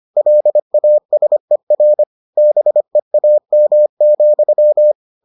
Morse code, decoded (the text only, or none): LASER BEAM,